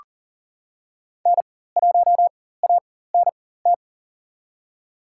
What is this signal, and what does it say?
Morse code, heard as a keyed tone, about 40 wpm, spaced as for 19 wpm, 700 Hz. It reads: N1ANT
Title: USA Call Signs - 5 Characters - 4x Character Spacing - 40wpm